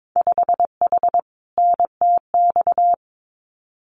Morse code, heard as a keyed tone, about 22 wpm, 700 Hz. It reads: 5H NTX
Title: Winter Field Day Exchanges Encore 22wpm